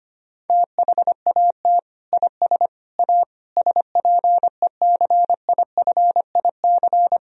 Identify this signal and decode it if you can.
Morse code, heard as a keyed tone, about 25 wpm, 700 Hz. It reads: THAT IS A SPECIFIC
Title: Sentences from Top 1000 Words - Review Part 3 - 25wpm